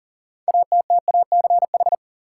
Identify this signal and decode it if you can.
Morse code, heard as a keyed tone, about 40 wpm, 700 Hz. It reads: ATTACH
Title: Six Letter Words - 40wpm